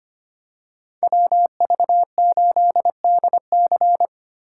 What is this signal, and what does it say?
Morse code, heard as a keyed tone, about 29 wpm, 700 Hz. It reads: WV8DC
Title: Speed Racing - USA 2x2 Call Signs 25wpm